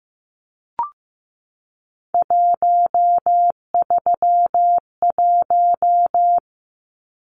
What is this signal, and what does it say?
Morse code, heard as a keyed tone, about 15 wpm, 700 Hz. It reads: 131
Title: Numbers - 3-Digit 15wpm